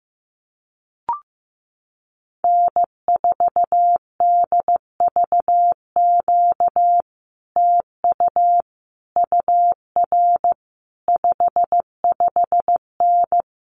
Morse code, heard as a keyed tone, about 15 wpm, 700 Hz. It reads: N4DVQ TU UR 55N
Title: Common QSO Phrases Encore 15wpm